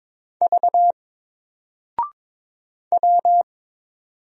Morse code, heard as a keyed tone, about 22 wpm, 700 Hz. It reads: V W
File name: Single-Letter-Number-22wpm